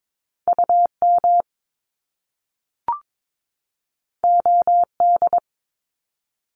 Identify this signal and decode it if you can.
Morse code, heard as a keyed tone, about 22 wpm, 700 Hz. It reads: UM OD
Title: Top Two Letter Combinations in Words 22wpm